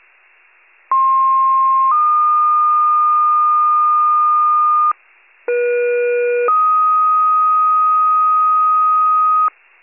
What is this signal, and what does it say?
Two-tone page, heard as a one-second tone, then a three-second tone, about 1060 then 1220 Hz, about 480 then 1250 Hz. It is heard from 0.9 to 9.5 s.
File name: pager / 23 FIRE / wcalert